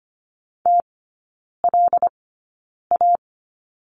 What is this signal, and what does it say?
Morse code, heard as a keyed tone, about 25 wpm, spaced as for 9 wpm, 700 Hz. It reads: TLA
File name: Lesson 14e - Number 2 - Three Characters 25wpm